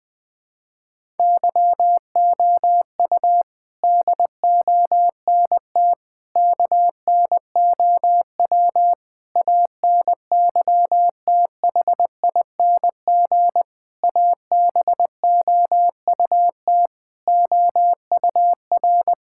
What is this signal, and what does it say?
Morse code, heard as a keyed tone, about 20 wpm, 700 Hz. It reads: YOU DONT KNOW ANYTHING ABOUT OUR
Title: Sentences from Top 1000 Words - Review Part 6 - 20wpm